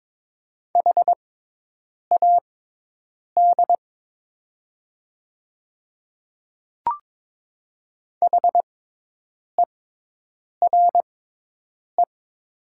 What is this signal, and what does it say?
Morse code, heard as a keyed tone, about 22 wpm, spaced as for 8 wpm, 700 Hz. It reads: HAD HERE
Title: 6x Character Spacing - Top 200 Words - 22wpm